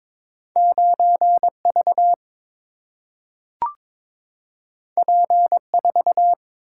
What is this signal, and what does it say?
Morse code, heard as a keyed tone, about 21 wpm, 700 Hz. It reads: 9V P4
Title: Single-Letter Single-Number 22wpm